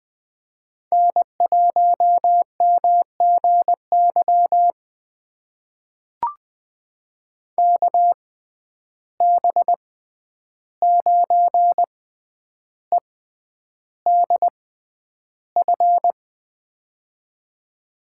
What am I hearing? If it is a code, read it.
Morse code, heard as a keyed tone, about 20 wpm, spaced as for 10 wpm, 700 Hz. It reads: N1MGY KB9EDF
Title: USA Call Signs - 6x Character Spacing - 20wpm